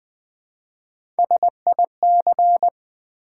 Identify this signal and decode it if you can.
Morse code, heard as a keyed tone, about 20 wpm, 700 Hz. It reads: SIC
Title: Build A Word - 1 Letter At A Time - 3 to 5-Letter Words - 20wpm